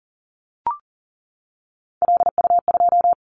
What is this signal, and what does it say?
Morse code, heard as a keyed tone, about 41 wpm, 700 Hz. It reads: LU2
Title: Callsign Prefixes for Top 100 DXCC - 40wpm